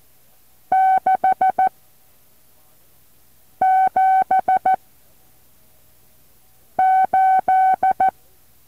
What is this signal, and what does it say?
Morse code, heard as a keyed tone, about 14 wpm, 760 Hz. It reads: 6 7 8
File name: intl morse code2